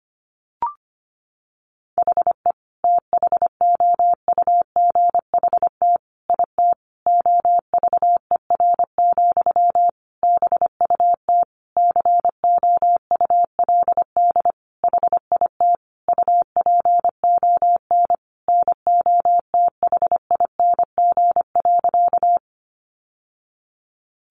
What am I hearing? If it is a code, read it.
Morse code, heard as a keyed tone, about 25 wpm, 700 Hz. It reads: HE THOUGHT IT OVER, BUT COULD HIT UPON NOTHING.
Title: Sentences from Top 1200 Words 25wpm